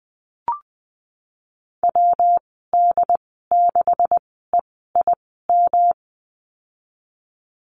Morse code, heard as a keyed tone, about 20 wpm, spaced as for 14 wpm, 700 Hz. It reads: WD6EIM
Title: 128 Lesson 21g - Number 6 - Call Signs 20wpm